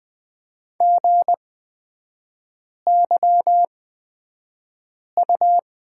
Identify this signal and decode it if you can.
Morse code, heard as a keyed tone, about 20 wpm, spaced as for 5 wpm, 700 Hz. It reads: GYU
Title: 236 Lesson 38e - Letter q - Three Characters 20wpm